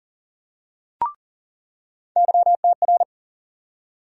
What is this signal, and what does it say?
Morse code, heard as a keyed tone, about 40 wpm, 700 Hz. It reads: YTR
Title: USA Call Sign Postfixes - 3 Letters 40wpm